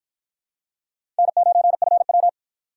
Morse code, heard as a keyed tone, about 51 wpm, 700 Hz. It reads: N9FW